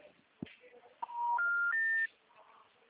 Special information tone.